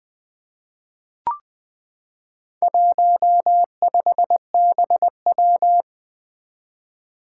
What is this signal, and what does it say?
Morse code, heard as a keyed tone, about 21 wpm, 700 Hz. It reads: 15BW
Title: Four Challenging Letter-Number Combinations 20wpm